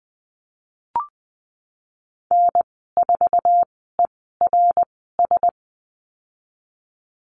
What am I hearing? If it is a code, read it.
Morse code, heard as a keyed tone, about 19 wpm, spaced as for 14 wpm, 700 Hz. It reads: N4ERS
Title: 069 Lesson 12g - Letter d - Call Signs 20wpm